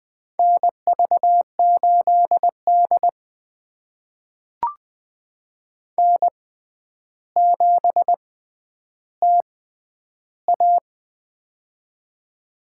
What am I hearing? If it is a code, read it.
Morse code, heard as a keyed tone, about 20 wpm, spaced as for 10 wpm, 700 Hz. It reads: NV8D N7TA
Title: USA Call Signs - 4 Characters - 6x Character Spacing - 20wpm